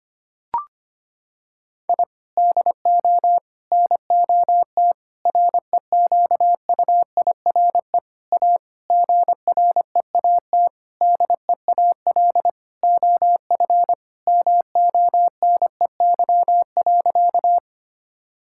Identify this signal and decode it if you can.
Morse code, heard as a keyed tone, about 25 wpm, 700 Hz. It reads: I DO NOT REQUIRE A GREAT DEAL OF MONEY.
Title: Sentences from Top 1000 Words - Review Part 6 - 25wpm